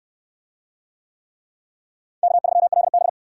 Morse code, heard as a keyed tone, about 69 wpm, 700 Hz. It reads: K3LB